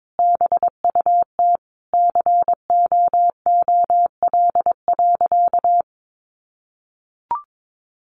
Morse code, heard as a keyed tone, about 22 wpm, 700 Hz. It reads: BUT COOL.
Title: Sentences from Top 1200 Words 22wpm